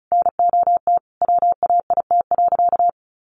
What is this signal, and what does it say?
Morse code, heard as a keyed tone, about 34 wpm, 700 Hz. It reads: NOT WAIT.
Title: Sentences from Top 1400 Words 35wpm